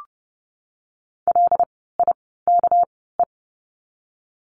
Morse code, heard as a keyed tone, about 30 wpm, spaced as for 17 wpm, 700 Hz. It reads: LIKE
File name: course / 30:10wpm / 230 Lesson 37f - Letter x - Words 30wpm